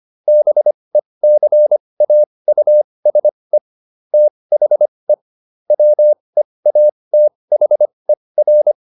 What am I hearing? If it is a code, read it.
Morse code, heard as a keyed tone, about 25 wpm, spaced as for 20 wpm, 600 Hz. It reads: BECAUSE THE WEATHER